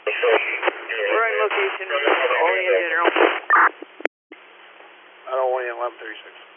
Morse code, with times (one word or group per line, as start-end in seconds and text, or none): none